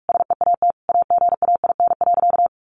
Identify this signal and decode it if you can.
Morse code, heard as a keyed tone, about 44 wpm, 700 Hz. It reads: SEAT AGAIN.
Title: Sentences from Top 1500 Words 45wpm